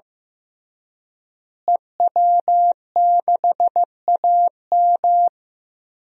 Morse code, heard as a keyed tone, about 15 wpm, 700 Hz. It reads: EW6AM
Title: DXCC-100 Call Signs - 5 Characters - Repeated 3x - 15wpm